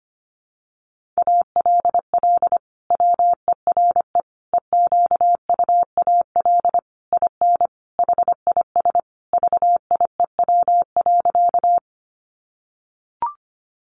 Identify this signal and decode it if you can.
Morse code, heard as a keyed tone, about 25 wpm, 700 Hz. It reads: ALL WERE EQUAL IN HIS VIEW.
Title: Sentences from Top 1400 Words 25wpm